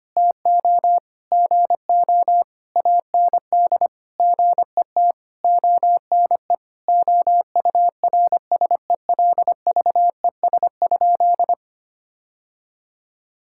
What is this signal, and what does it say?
Morse code, heard as a keyed tone, about 25 wpm, 700 Hz. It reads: TO GO AND GET ONE OURSELVES?
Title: Sentences from Top 1600 Words 25wpm